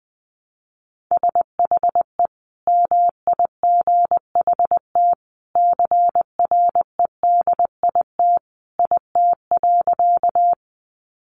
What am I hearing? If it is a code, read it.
Morse code, heard as a keyed tone, about 20 wpm, 700 Hz. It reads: SHE MIGHT CREDIT IT.
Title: Sentences from Top 1000 Words 20wpm